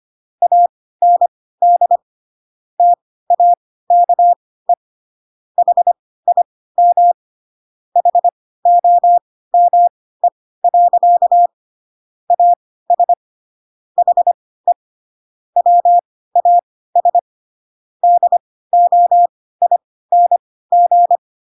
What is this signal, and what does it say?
Morse code, heard as a keyed tone, about 25 wpm, spaced as for 16 wpm, 700 Hz. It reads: AND TAKE HIM HOME. AS HE WAS DOING
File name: Pinocchio / Pinocchio ch3 10 25 ik1hge